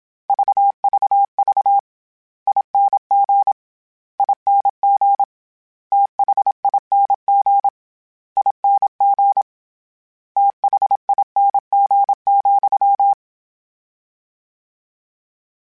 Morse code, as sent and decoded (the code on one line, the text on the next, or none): ...- ...- ...- / .. -. --. / .. -. --. / - .... .. -. --. / .. -. --. / - .... .. -. --. --..--
VVV ING ING THING ING THING,